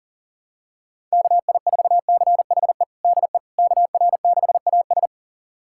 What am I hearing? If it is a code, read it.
Morse code, heard as a keyed tone, about 40 wpm, 700 Hz. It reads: KI4CHE DE KR6AS
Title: USA Callsign DE Callsign 40wpm